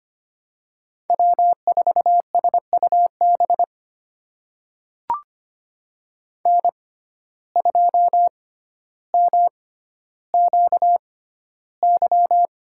Morse code, heard as a keyed tone, about 25 wpm, spaced as for 13 wpm, 700 Hz. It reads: W4SUB N2MQY
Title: USA Call Signs - 5 Characters - 6x Character Spacing - 25wpm